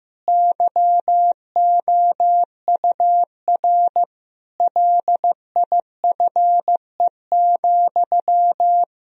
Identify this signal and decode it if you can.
Morse code, heard as a keyed tone, about 15 wpm, 700 Hz. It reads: YOUR LIFE,
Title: Sentences from Top 1000 Words - Review Part 3 - 15wpm